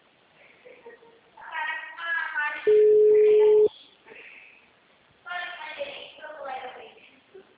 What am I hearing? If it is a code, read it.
Ringback tone (Europe, 425 Hz), with speech.
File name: out- -2023- - - .52819